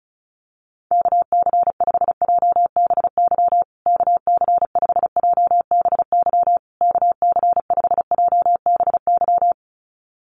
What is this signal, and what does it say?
Morse code, heard as a keyed tone, about 34 wpm, 700 Hz. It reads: KC5JBY KC5JBY KC5JBY
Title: USA Call Signs Repeated 3x 35wpm